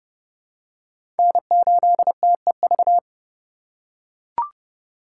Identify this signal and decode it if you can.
Morse code, heard as a keyed tone, about 30 wpm, 700 Hz. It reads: N8TEV
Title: USA Call Signs - 5 Characters - 2x Character Spacing - 30wpm